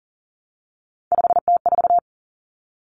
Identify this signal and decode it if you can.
Morse code, heard as a keyed tone, about 40 wpm, 700 Hz. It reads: 5T4
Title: Common Cut Numbers - 3-Digit 40wpm